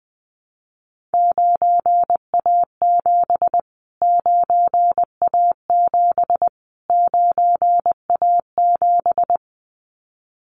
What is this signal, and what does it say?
Morse code, heard as a keyed tone, about 20 wpm, 700 Hz. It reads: 9A7 9A7 9A7
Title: Callsign Prefixes for Top 25 DXCC - Repeated 3x - 20wpm